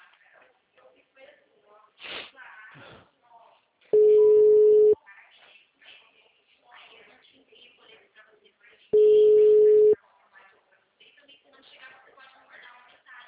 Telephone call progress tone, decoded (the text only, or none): ringback tone